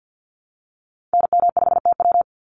Morse code, heard as a keyed tone, about 50 wpm, 700 Hz. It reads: NM5TW
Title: NA SOTA Activator Callsigns 50wpm